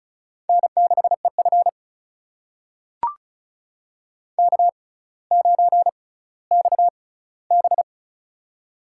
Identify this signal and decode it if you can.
Morse code, heard as a keyed tone, about 36 wpm, spaced as for 18 wpm, 690 Hz. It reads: N6EF K9XB
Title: USA Call Signs - 4 Characters - 6x Character Spacing - 35wpm